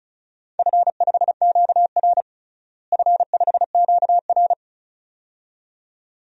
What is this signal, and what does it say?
Morse code, heard as a keyed tone, about 34 wpm, 700 Hz. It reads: F5QR F5QR
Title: DXCC-100 Call Signs - 4 Characters - Repeated 2x - 35wpm